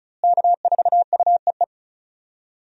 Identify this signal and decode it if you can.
Morse code, heard as a keyed tone, about 36 wpm, 700 Hz. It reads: K4UEE
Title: USA Call Signs - 5 Characters - Repeated 3x - 35wpm